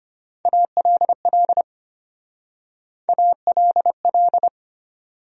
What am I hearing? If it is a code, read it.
Morse code, heard as a keyed tone, about 32 wpm, 700 Hz. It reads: ALL ALL